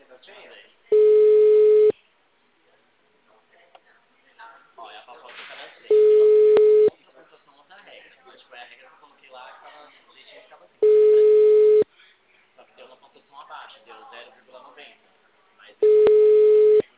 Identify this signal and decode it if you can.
Ringback tone (Europe, 425 Hz), with speech.